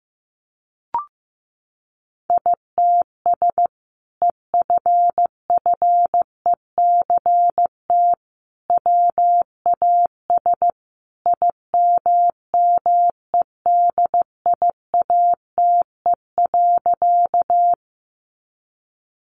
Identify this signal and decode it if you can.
Morse code, heard as a keyed tone, about 15 wpm, 700 Hz. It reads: ITS EFFECT WAS IMMEDIATE.
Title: Sentences from Top 1200 Words 15wpm